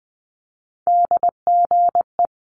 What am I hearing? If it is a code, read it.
Morse code, heard as a keyed tone, about 20 wpm, 700 Hz. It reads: DGE
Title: Multi-letter Phonemes Quick Fire 20wpm